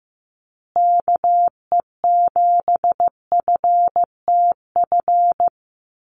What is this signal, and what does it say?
Morse code, heard as a keyed tone, about 15 wpm, 700 Hz. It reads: KE7FTF
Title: USA Call Signs Repeated 2x 15wpm